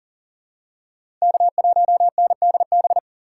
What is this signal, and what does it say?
Morse code, heard as a keyed tone, about 40 wpm, 700 Hz. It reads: K1NDB
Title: USA Call Signs - 6x Character Spacing - 40wpm